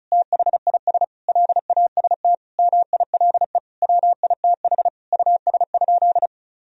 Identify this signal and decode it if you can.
Morse code, heard as a keyed tone, about 36 wpm, 700 Hz. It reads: THIS LAST MILE WITH US?
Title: Sentences from Top 1000 Words - Review Part 1 - 35wpm